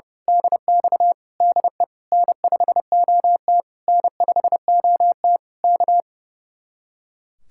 Morse code, heard as a keyed tone, about 29 wpm, 700 Hz. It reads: DX DE N5OT N5OT K
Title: Speed-Racing Common QSO Phrases Encore - 30wpm